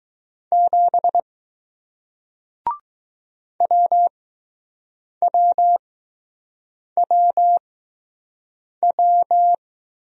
Morse code, heard as a keyed tone, about 20 wpm, spaced as for 6 wpm, 700 Hz. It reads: 7 WWWW